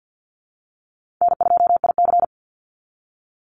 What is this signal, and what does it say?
Morse code, heard as a keyed tone, about 48 wpm, 700 Hz. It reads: N2IC
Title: Easy Call Signs Repeated 3x - 50wpm